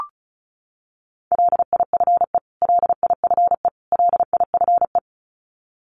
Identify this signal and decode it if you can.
Morse code, heard as a keyed tone, about 34 wpm, 700 Hz. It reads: LIFE LIFE LIFE
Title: Top 300 Words Repeated 35wpm